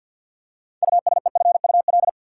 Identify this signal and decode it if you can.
Morse code, heard as a keyed tone, about 53 wpm, 700 Hz. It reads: USEFUL
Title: Six Letter Words - 50wpm